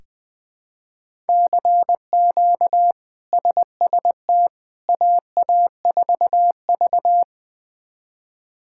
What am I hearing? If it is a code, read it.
Morse code, heard as a keyed tone, about 20 wpm, 700 Hz. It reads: CQ SST AA4V
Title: K1USN SST QSO Practice - 20wpm